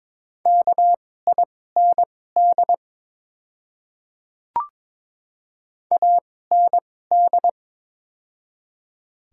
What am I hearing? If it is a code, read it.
Morse code, heard as a keyed tone, about 22 wpm, spaced as for 16 wpm, 700 Hz. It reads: KIND AND
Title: 2x Character Spacing - Top 300 Words - 22wpm